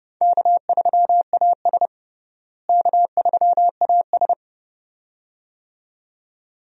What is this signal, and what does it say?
Morse code, heard as a keyed tone, about 30 wpm, 700 Hz. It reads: K3AS K3AS
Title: USA Call Signs - 4 Characters - Repeated 2x - 30wpm